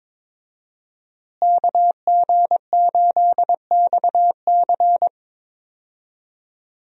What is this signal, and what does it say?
Morse code, heard as a keyed tone, about 22 wpm, 700 Hz. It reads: KG8XC
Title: USA Call Signs - 5 Characters - Repeated 3x - 22wpm